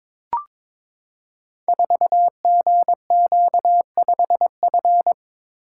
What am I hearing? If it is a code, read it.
Morse code, heard as a keyed tone, about 23 wpm, 700 Hz. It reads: 4GQ5F
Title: Code Groups - 5 Letter:Numbers - 22wpm